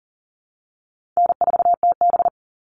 Morse code, heard as a keyed tone, about 40 wpm, 700 Hz. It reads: N4TB
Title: Easy Call Signs Repeated 2x - 40wpm